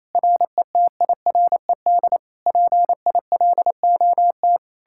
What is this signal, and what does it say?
Morse code, heard as a keyed tone, about 37 wpm, 700 Hz. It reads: RETIRED PILOT